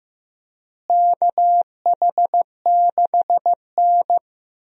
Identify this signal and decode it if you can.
Morse code, heard as a keyed tone, about 15 wpm, 700 Hz. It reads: KH6N
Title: DXCC-100 Call Signs - 4 Characters - Repeated 3x - 15wpm